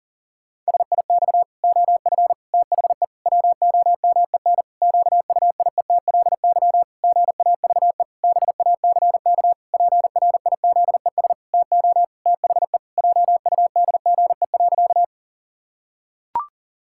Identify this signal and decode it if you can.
Morse code, heard as a keyed tone, about 40 wpm, 700 Hz. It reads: SIX OF THE WOMEN QUIETLY GAVE BACK PRIZES TO THE JUDGE.
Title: Pangram Challenge v2 - 40wpm